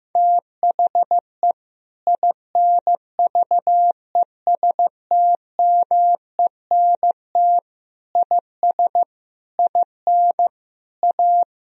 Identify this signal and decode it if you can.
Morse code, heard as a keyed tone, about 16 wpm, 700 Hz. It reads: THE INVESTMENT IS IN A
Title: Sentences from Top 1000 Words - Review Part 4 - 15wpm